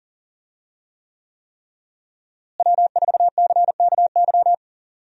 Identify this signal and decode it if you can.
Morse code, heard as a keyed tone, about 40 wpm, 700 Hz. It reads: W4CKY
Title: USA Call Signs - 5 Characters - Repeated 3x - 40wpm